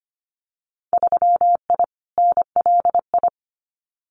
Morse code, heard as a keyed tone, about 25 wpm, 690 Hz. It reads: 3I NLI
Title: Winter Field Day Exchanges Encore 25wpm